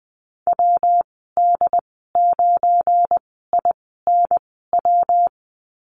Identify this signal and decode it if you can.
Morse code, heard as a keyed tone, about 20 wpm, spaced as for 14 wpm, 700 Hz. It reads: WD9INW
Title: 207 Lesson 33g - Letter j - Call Signs 20wpm